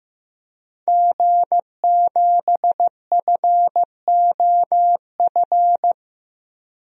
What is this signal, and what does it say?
Morse code, heard as a keyed tone, about 15 wpm, 700 Hz. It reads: G7FOF
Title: DXCC-100 Call Signs - 5 Characters - 15wpm